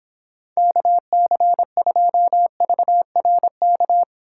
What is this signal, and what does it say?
Morse code, heard as a keyed tone, about 26 wpm, 700 Hz. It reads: KC2VRK